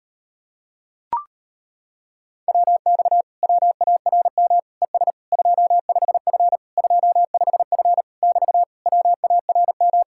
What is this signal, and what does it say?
Morse code, heard as a keyed tone, about 37 wpm, 700 Hz. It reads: WX WARM ES 25F 25F <BT> WARM